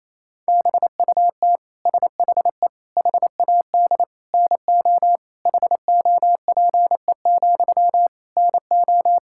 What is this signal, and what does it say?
Morse code, heard as a keyed tone, about 28 wpm, 700 Hz. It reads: BUT SHE HAD NO HOPE, NO
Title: Sentences from Top 1600 Words 28wpm